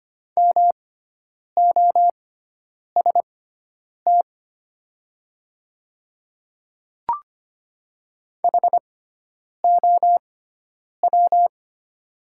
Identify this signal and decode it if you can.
Morse code, heard as a keyed tone, about 25 wpm, spaced as for 9 wpm, 690 Hz. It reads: MOST HOW
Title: 6x Character Spacing - Top 100 Words - 25wpm